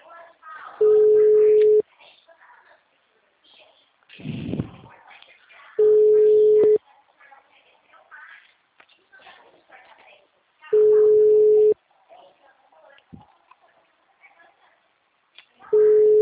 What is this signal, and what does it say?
Ringback tone (Europe, 425 Hz), with speech.